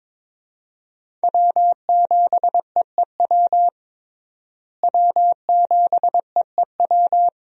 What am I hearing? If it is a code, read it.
Morse code, heard as a keyed tone, about 21 wpm, 700 Hz. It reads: W7EEW W7EEW
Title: USA Call Signs - 5 Characters - Repeated 2x - 22wpm